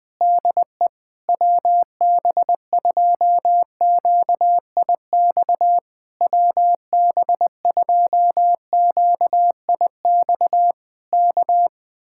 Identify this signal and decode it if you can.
Morse code, heard as a keyed tone, about 20 wpm, 700 Hz. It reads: DE WB2QIX WB2QIX K
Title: Speed-Racing Common QSO Phrases Encore - 20wpm